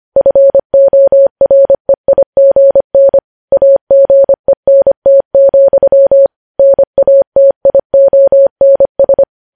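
Morse code, heard as a keyed tone, about 25 wpm, 550 Hz. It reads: FOREIGN AGENT, NATIONS
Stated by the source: Morse news headlines sent at 25 wpm.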